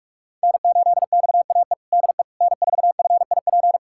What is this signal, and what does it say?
Morse code, heard as a keyed tone, about 46 wpm, 700 Hz. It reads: N8XAE DE N4FIP
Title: USA Callsign DE Callsign - 5 Characters - 45wpm